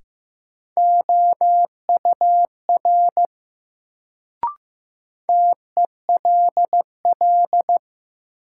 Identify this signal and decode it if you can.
Morse code, heard as a keyed tone, about 15 wpm, 700 Hz. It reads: OUR TELL
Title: Sets of 2 Words - New Word Drill - 15wpm